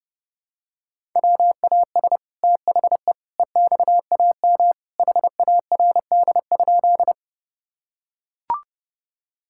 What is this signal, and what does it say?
Morse code, heard as a keyed tone, about 30 wpm, 700 Hz. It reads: WAS THE EXAM HARD?
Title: Sentences from Top 1000 Words - Review Part 5 - 30wpm